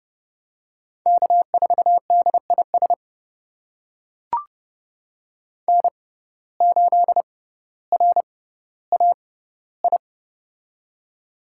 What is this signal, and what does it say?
Morse code, heard as a keyed tone, about 31 wpm, spaced as for 16 wpm, 700 Hz. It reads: K4DIS N8RAI
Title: USA Call Signs - 5 Characters - 6x Character Spacing - 30wpm